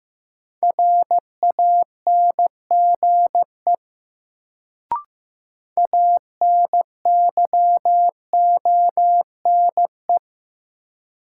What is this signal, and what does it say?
Morse code, heard as a keyed tone, about 15 wpm, 700 Hz. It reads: RANGE ANYONE
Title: Sentences from Top 900 Words - New Word Drill - 15wpm